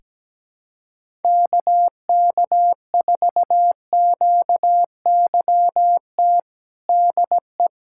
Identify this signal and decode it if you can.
Morse code, heard as a keyed tone, about 17 wpm, 700 Hz. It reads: KK4QYT DE
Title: USA Callsign DE Callsign - 6 Characters - 17wpm